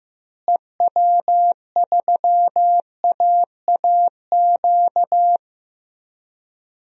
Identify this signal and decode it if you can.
Morse code, heard as a keyed tone, about 15 wpm, 700 Hz. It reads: EW3AAQ
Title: DXCC-100 Call Signs - 6 Characters - Repeated 2x - 15wpm